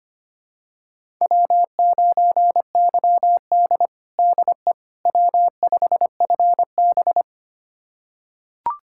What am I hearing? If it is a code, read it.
Morse code, heard as a keyed tone, about 25 wpm, 700 Hz. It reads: W9YD DE W5FB
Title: USA Callsign DE Callsign - 4 Characters - 25wpm